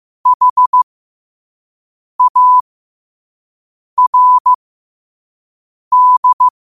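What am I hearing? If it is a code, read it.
Morse code, heard as a keyed tone, about 15 wpm, spaced as for 5 wpm, 1000 Hz. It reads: HARD
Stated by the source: Morse practice quote sent at 5 wpm.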